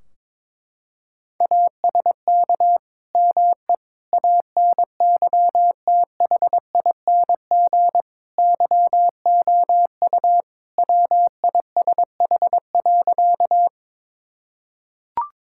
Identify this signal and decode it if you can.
Morse code, heard as a keyed tone, about 22 wpm, 700 Hz. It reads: ASK ME ANYTHING YOU WISH.
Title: Sentences from Top 1600 Words 22wpm